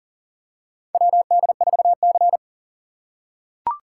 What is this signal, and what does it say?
Morse code, heard as a keyed tone, about 40 wpm, 700 Hz. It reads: WD4C
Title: USA Call Signs - 4 Characters - 6x Character Spacing - 40wpm